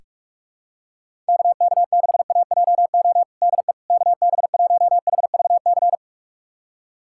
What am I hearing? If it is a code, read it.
Morse code, heard as a keyed tone, about 46 wpm, 700 Hz. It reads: KK6AJO DE KB1HVC
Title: USA Callsign DE Callsign 45wpm